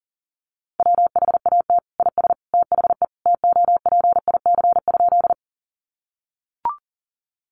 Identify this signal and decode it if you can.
Morse code, heard as a keyed tone, about 40 wpm, 700 Hz. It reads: WHAT IS THE TOPIC?
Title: Sentences from Top 1200 Words 40wpm